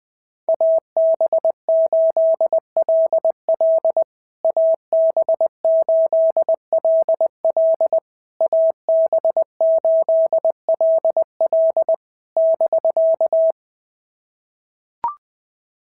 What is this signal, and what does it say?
Morse code, heard as a keyed tone, about 20 wpm, 650 Hz. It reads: AB8LL AB8LL AB8LL <BK>
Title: POTA QSO Practice - 20wpm